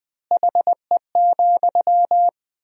Morse code, heard as a keyed tone, about 20 wpm, 700 Hz. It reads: HE,
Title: 4x Word-Spacing - Sentences from Top 500 Words - 20wpm